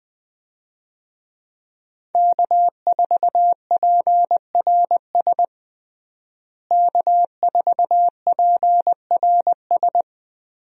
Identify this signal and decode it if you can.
Morse code, heard as a keyed tone, about 20 wpm, 700 Hz. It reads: K4PRS K4PRS
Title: USA Call Signs - 5 Characters - Repeated 3x - 20wpm